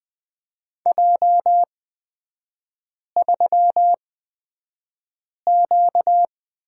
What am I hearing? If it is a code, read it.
Morse code, heard as a keyed tone, about 20 wpm, spaced as for 5 wpm, 700 Hz. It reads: J3Q
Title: 255 Lesson 41e - Period - Three Characters 20wpm